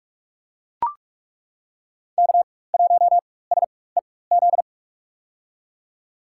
Morse code, heard as a keyed tone, about 41 wpm, spaced as for 20 wpm, 700 Hz. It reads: K1SEZ
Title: USA Call Signs - 5 Characters - 4x Character Spacing - 45wpm